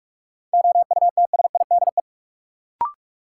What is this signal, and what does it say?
Morse code, heard as a keyed tone, about 42 wpm, 700 Hz. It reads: OUTSIDE
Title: Sentences from Top 900 Words - New Word Drill - 45wpm